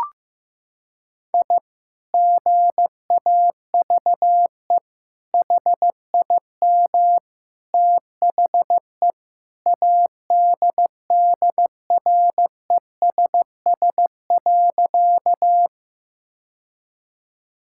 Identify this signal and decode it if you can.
Morse code, heard as a keyed tone, about 16 wpm, 700 Hz. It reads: I GAVE HIM THE ADDRESS.
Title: Sentences from Top 1000 Words - Review Part 6 - 15wpm